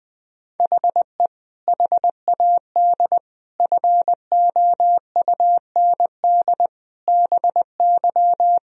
Morse code, heard as a keyed tone, about 20 wpm, 700 Hz. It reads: HE HAD FOUND BY
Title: Sentences from Top 1000 Words - Review Part 3 - 20wpm